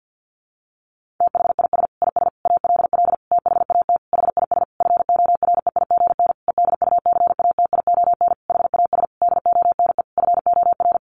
Morse code, heard as a keyed tone, about 50 wpm, 700 Hz. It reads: THIS IS ALL THAT HIS FOREIGN EDUCATION HAS DONE FOR
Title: Sentences from Top 1400 Words 50wpm